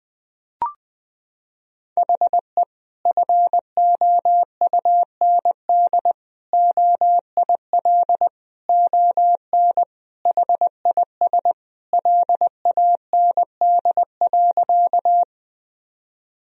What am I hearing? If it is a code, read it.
Morse code, heard as a keyed tone, about 21 wpm, 700 Hz. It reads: HE FOUND OIL ON HIS LAND.
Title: Sentences from Top 1000 Words - Review Part 4 - 20wpm